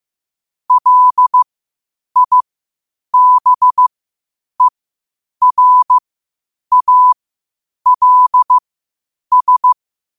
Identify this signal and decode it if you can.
Morse code, heard as a keyed tone, about 15 wpm, spaced as for 8 wpm, 1000 Hz. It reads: LIBERALS
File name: qotdfiles / 8wpm